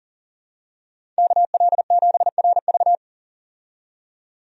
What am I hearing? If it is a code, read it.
Morse code, heard as a keyed tone, about 37 wpm, 700 Hz. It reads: KL7RV
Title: USA Call Signs - 5 Characters - 40wpm